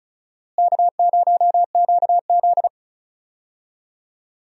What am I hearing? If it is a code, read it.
Morse code, heard as a keyed tone, about 36 wpm, 700 Hz. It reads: K0QZ
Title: USA Call Signs - 4 Characters - Repeated 3x - 35wpm